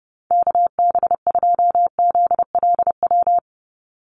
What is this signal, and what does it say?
Morse code, heard as a keyed tone, about 30 wpm, 700 Hz. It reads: KB2ZLW
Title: USA Call Signs 30wpm